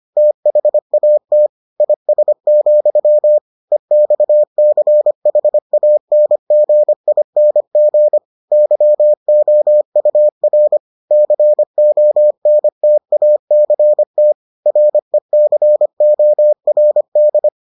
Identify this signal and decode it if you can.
Morse code, heard as a keyed tone, about 25 wpm, 600 Hz. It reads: THAT IS, EXCHANGING YOUR CONTACT RECORD